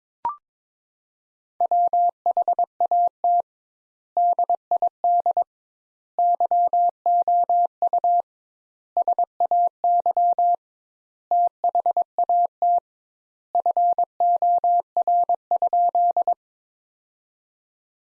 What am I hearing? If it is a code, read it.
Morse code, heard as a keyed tone, about 22 wpm, 700 Hz. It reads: WHAT DID YOU SAY THAT FOR?
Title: Sentences Top 200 Words - 22wpm